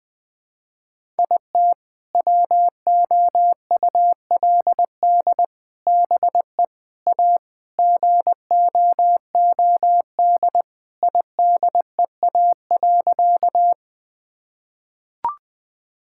Morse code, heard as a keyed tone, about 21 wpm, 700 Hz. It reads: IT WOULD BE A GOOD IDEA.
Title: Sentences with 3-Word Phrases - Spoken Beginning - 20wpm